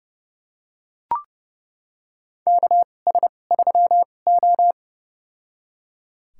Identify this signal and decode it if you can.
Morse code, heard as a keyed tone, about 31 wpm, spaced as for 22 wpm, 700 Hz. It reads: KS3O
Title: USA Call Signs - 2x Character Spacing - 30wpm